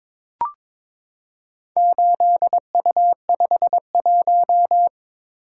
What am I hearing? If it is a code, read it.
Morse code, heard as a keyed tone, about 22 wpm, 700 Hz. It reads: 8U51
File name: practice / 22wpm / Four Letter-Number Combinations - 22wpm